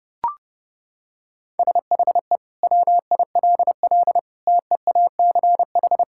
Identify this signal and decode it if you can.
Morse code, heard as a keyed tone, about 30 wpm, 700 Hz. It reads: SHE WILL TEACH
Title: Sentences from Top 1000 Words - Review Part 6 - 30wpm